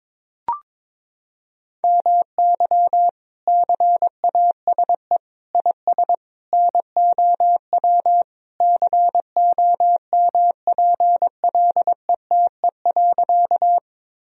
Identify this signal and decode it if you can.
Morse code, heard as a keyed tone, about 21 wpm, 700 Hz. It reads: MY CASE IS NOW COMPLETE.
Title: Sentences from Top 1000 Words - Review Part 2 - 22wpm